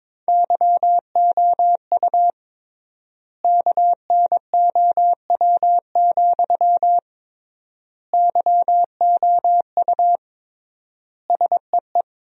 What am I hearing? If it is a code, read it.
Morse code, heard as a keyed tone, about 22 wpm, 700 Hz. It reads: YOU KNOW, YOU SEE
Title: 3x Word-Spacing - Sentences from Top 200 Words - 22wpm